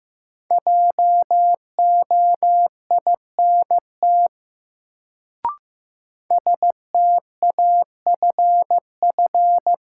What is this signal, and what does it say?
Morse code, heard as a keyed tone, about 15 wpm, 700 Hz. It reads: JOINT STAFF
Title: Sets of 5 Words Encore - New Word Drill - 15wpm